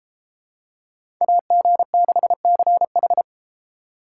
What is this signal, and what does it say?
Morse code, heard as a keyed tone, about 30 wpm, 700 Hz. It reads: AG6CH